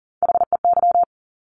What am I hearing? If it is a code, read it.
Morse code, heard as a keyed tone, about 41 wpm, 700 Hz. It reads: HEY
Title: Top Three Letter Combinations in Words 40wpm